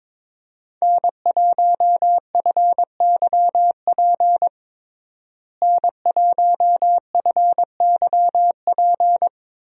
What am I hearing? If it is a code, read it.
Morse code, heard as a keyed tone, about 22 wpm, 700 Hz. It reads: N1FYP N1FYP
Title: USA Call Signs - 5 Characters - Repeated 2x - 22wpm